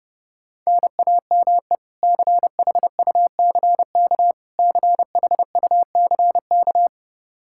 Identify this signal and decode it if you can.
Morse code, heard as a keyed tone, about 32 wpm, 700 Hz. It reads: NAME CHUCK CHUCK
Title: K1USN SST QSO Practice 30wpm